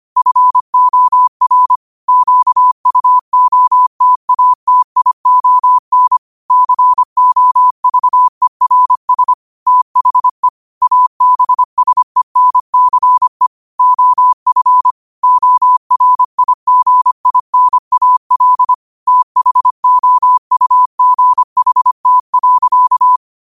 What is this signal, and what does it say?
Morse code, heard as a keyed tone, about 25 wpm, 1000 Hz. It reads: FOR QUOTATION COVERS THE ABSENCE OF ORIGINAL THOUGHT.
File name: qotdfiles / 25wpm